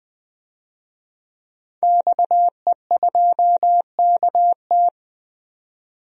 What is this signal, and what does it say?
Morse code, heard as a keyed tone, about 21 wpm, 700 Hz. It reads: XE2KT
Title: DXCC-100 Call Signs - 5 Characters - 20wpm